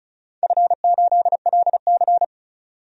Morse code, heard as a keyed tone, about 34 wpm, 700 Hz. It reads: F8LC
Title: DXCC-100 Call Signs - 4 Characters - 35wpm